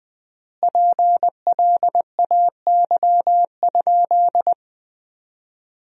Morse code, heard as a keyed tone, about 21 wpm, 700 Hz. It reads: PLAY?
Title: 3x Word-Spacing - Sentences from Top 400 Words - 20wpm